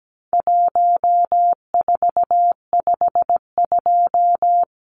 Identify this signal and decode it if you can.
Morse code, heard as a keyed tone, about 17 wpm, 700 Hz. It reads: 1452
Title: Numbers - Contest Style - 4-Digit 17wpm